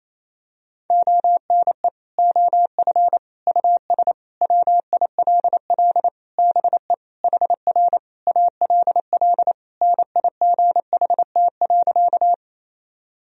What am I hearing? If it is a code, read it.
Morse code, heard as a keyed tone, about 28 wpm, 700 Hz. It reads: ONE OF US WILL BE HR ALL NIGHT.
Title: Sentences from Top 300 Words and Abbreviations 28wpm